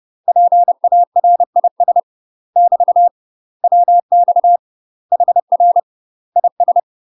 Morse code, heard as a keyed tone, about 30 wpm, 700 Hz. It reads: PARIS <BT> WX HR IS
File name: UK MASTER QSOs TRAINING / 30WPM / UK MASTER QSOs 0002